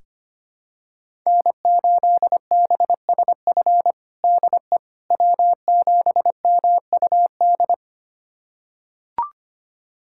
Morse code, heard as a keyed tone, about 25 wpm, 700 Hz. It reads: N8BSF DE W7MUD
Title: USA Callsign DE Callsign - 5 Characters - 25wpm